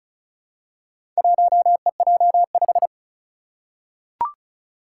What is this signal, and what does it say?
Morse code, heard as a keyed tone, about 36 wpm, 700 Hz. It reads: 1EJ5
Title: Four Challenging Letter-Number Combinations 35wpm